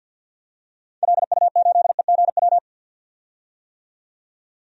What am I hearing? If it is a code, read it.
Morse code, heard as a keyed tone, about 50 wpm, 700 Hz. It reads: LU8EGW
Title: DXCC-100 Call Signs - 6 Characters - 50wpm